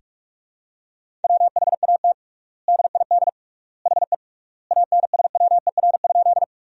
Morse code, heard as a keyed tone, about 48 wpm, 700 Hz. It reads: WHAT DID HE ANSWER?
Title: 3x Word-Spacing - Sentences from Top 300 Words - 45wpm